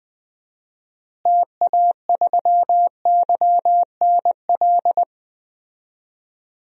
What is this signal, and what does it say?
Morse code, heard as a keyed tone, about 20 wpm, 700 Hz. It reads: TA3YNL
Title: DXCC-100 Call Signs - 6 Characters - Repeated 2x - 20wpm